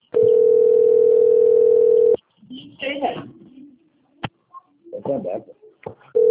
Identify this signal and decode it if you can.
Ringback tone (North America), 440 + 480 Hz, with speech.
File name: rg-1600-unknown- - - .37625